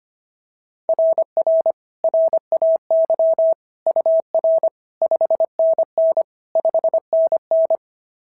Morse code, heard as a keyed tone, about 25 wpm, 690 Hz. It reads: RR RAY UR 5NN 5NN
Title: SOTA QSO Practice 25wpm